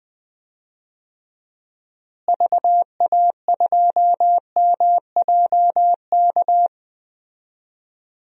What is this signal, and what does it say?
Morse code, heard as a keyed tone, about 20 wpm, 700 Hz. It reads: VA2MJK
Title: DXCC-100 Call Signs - 6 Characters - Repeated 3x - 20wpm